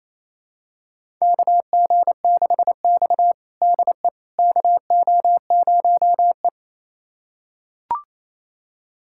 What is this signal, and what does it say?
Morse code, heard as a keyed tone, about 28 wpm, 700 Hz. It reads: KG6X DE KO0E
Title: USA Callsign DE Callsign - 4 Characters - 28wpm